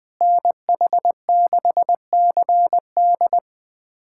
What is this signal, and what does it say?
Morse code, heard as a keyed tone, about 20 wpm, 700 Hz. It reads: NH6CD
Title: USA Callsign DE Callsign - 5 Characters - 20wpm